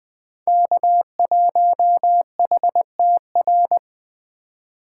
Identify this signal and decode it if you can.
Morse code, heard as a keyed tone, about 20 wpm, 700 Hz. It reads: K1HTR